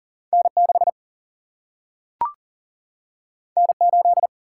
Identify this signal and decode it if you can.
Morse code, heard as a keyed tone, about 40 wpm, 700 Hz. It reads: N 6 N 8
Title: Single-Letter Single-Number 40wpm